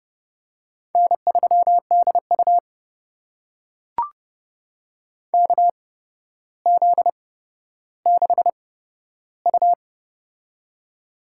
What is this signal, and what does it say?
Morse code, heard as a keyed tone, about 31 wpm, spaced as for 13 wpm, 700 Hz. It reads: N3DU KZ6U
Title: USA Call Signs - 4 Characters - 8x Character Spacing - 30wpm